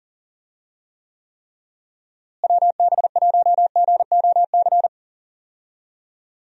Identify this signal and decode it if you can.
Morse code, heard as a keyed tone, about 40 wpm, 700 Hz. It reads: WB1GOC
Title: USA Call Signs - 6 Characters - Repeated 3x - 40wpm